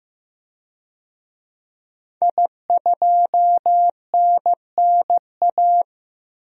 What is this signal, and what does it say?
Morse code, heard as a keyed tone, about 15 wpm, 700 Hz. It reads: I2NNA
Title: DXCC-100 Call Signs - 5 Characters - 15wpm